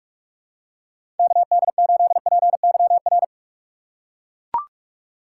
Morse code, heard as a keyed tone, about 44 wpm, 700 Hz. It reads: KD8PYR